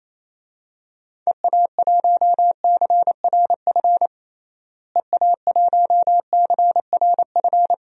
Morse code, heard as a keyed tone, about 28 wpm, 700 Hz. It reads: EA1CRF EA1CRF
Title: DXCC-100 Call Signs - 6 Characters - Repeated 2x - 28wpm